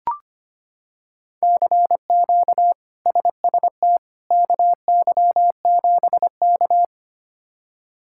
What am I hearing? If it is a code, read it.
Morse code, heard as a keyed tone, about 25 wpm, 700 Hz. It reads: CQ SST KY7K
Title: K1USN SST QSO Practice - 25wpm